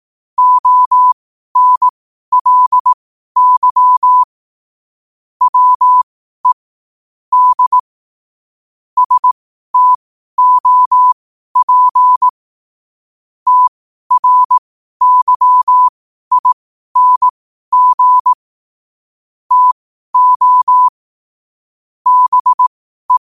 Morse code, heard as a keyed tone, about 18 wpm, spaced as for 12 wpm, 1000 Hz. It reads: ONLY WE D STOP TRYING TO BE
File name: qotdfiles / 12wpm